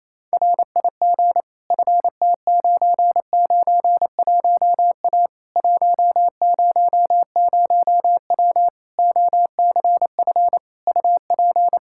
Morse code, heard as a keyed tone, about 28 wpm, 700 Hz. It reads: RIG FT991A 100W OCF UP
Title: Practice QSO - Exchange 2 - Info - 28wpm